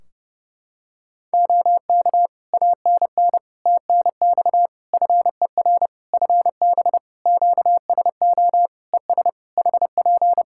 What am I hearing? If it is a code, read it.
Morse code, heard as a keyed tone, about 30 wpm, 700 Hz. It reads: OK ANN TNX FER FB QSO ES HP
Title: Practice QSO - Exchange 4 - Ending - 30wpm